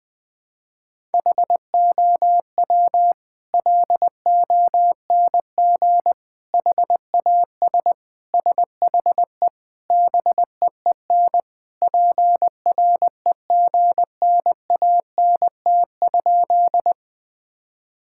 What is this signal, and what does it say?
Morse code, heard as a keyed tone, about 20 wpm, 700 Hz. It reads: HOW LONG HAS SHE BEEN PREGNANT?
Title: Sentences from Top 1000 Words - Review Part 6 - 20wpm